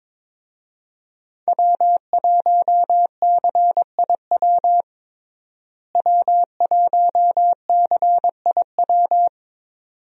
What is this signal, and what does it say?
Morse code, heard as a keyed tone, about 22 wpm, 700 Hz. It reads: W1CIW W1CIW
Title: USA Call Signs - 5 Characters - Repeated 2x - 22wpm